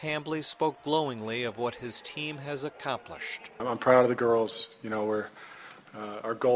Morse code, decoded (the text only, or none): none